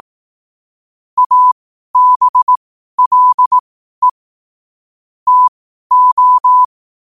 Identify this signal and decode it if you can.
Morse code, heard as a keyed tone, about 18 wpm, spaced as for 12 wpm, 1000 Hz. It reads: ABLE TO
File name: qotdfiles / 12wpm